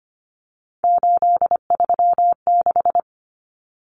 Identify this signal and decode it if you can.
Morse code, heard as a keyed tone, about 25 wpm, 700 Hz. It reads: 836
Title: Numbers - 3-Digit 25wpm